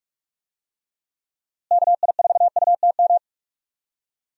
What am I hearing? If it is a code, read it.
Morse code, heard as a keyed tone, about 43 wpm, 700 Hz. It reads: KI4UTM
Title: USA Call Signs - 6 Characters - Repeated 3x - 45wpm